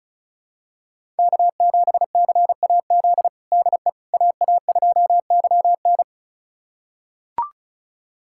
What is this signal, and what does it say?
Morse code, heard as a keyed tone, about 36 wpm, 700 Hz. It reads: K7CAZ DE AA2YN
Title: USA Callsign DE Callsign - 5 Characters - 35wpm